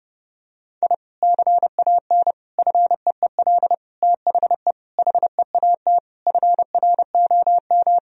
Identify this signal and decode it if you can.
Morse code, heard as a keyed tone, about 30 wpm, 700 Hz. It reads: I CAN FEEL THE HEAT FROM
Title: Sentences from Top 1000 Words - Review Part 3 - 30wpm